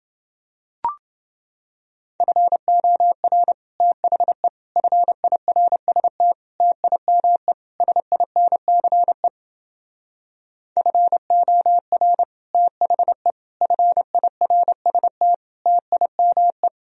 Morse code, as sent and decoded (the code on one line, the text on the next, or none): ..-. --- .-. / - .... / ..-. .. .-. ... - / - .. -- . / ... .. -. -.-. / ..-. --- .-. / - .... / ..-. .. .-. ... - / - .. -- .
FOR THE FIRST TIME SINCE FOR THE FIRST TIME